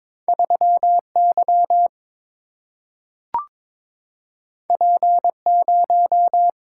Morse code, heard as a keyed tone, about 22 wpm, 700 Hz. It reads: 3 Y P 0
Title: Single-Letter Single-Number 22wpm